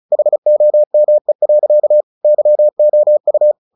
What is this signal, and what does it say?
Morse code, heard as a keyed tone, about 34 wpm, 600 Hz. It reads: HOME. YOU
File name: the old man and the sea 2 35wpm